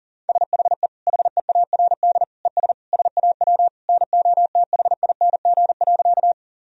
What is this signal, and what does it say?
Morse code, heard as a keyed tone, about 37 wpm, 700 Hz. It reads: SHE HEARD ES SAW NOTHING.
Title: Sentences from Top 500 Words and Abbreviations 40wpm